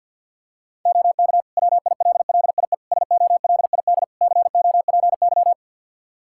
Morse code, heard as a keyed tone, about 52 wpm, 700 Hz. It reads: OK WILLIE SOLID COPY
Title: Practice QSO - Exchange 3 - More Info - 50wpm